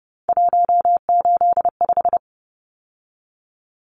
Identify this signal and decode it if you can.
Morse code, heard as a keyed tone, about 29 wpm, 700 Hz. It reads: 185
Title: Numbers - 3-Digit 30wpm